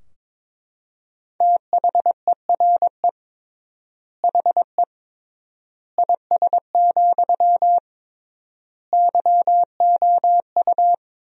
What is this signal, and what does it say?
Morse code, heard as a keyed tone, about 22 wpm, 700 Hz. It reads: THERE HE IS, YOU
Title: 3x Word-Spacing - Sentences from Top 100 Words - 22wpm